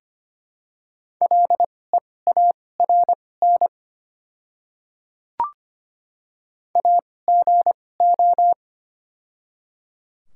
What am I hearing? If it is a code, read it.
Morse code, heard as a keyed tone, about 25 wpm, spaced as for 18 wpm, 700 Hz. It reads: LEARN AGO
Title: 2x Character Spacing - Top 500 Words - 25wpm